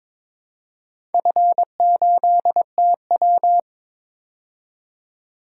Morse code, heard as a keyed tone, about 22 wpm, 690 Hz. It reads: F8TW
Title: DXCC-100 Call Signs - 4 Characters - 22wpm